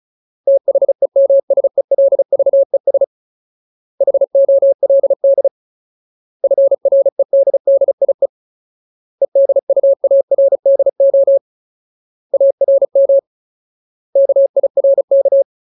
Morse code, heard as a keyed tone, about 36 wpm, 550 Hz. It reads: THEMSELVES HOLD FREDDIE EDUARDO ARM KIRK
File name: Ultimate Word List / 35WPM Ultimate